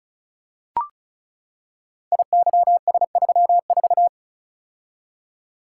Morse code, heard as a keyed tone, about 36 wpm, 700 Hz. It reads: IYS34
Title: Code Groups - 5 Letter:Numbers - 35wpm